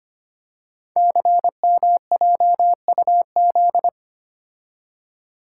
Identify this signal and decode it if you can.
Morse code, heard as a keyed tone, about 25 wpm, 700 Hz. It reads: CMJUZ
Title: Code Groups - 5 Letters - 25wpm